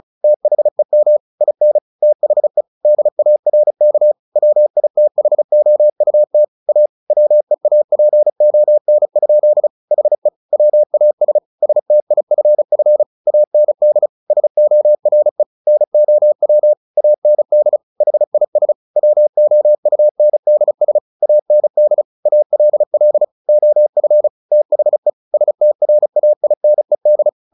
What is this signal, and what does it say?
Morse code, heard as a keyed tone, about 34 wpm, 600 Hz. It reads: THEM IN THE DARK WITHOUT A WEAPON? HE WAS STIFF AND SORE NOW AND HIS WOUNDS AND ALL OF THE STRAINED
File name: the old man and the sea 2 35wpm